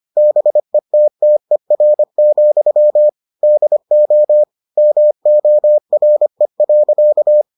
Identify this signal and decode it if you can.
Morse code, heard as a keyed tone, about 25 wpm, 600 Hz. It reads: BETTER, DO MORE.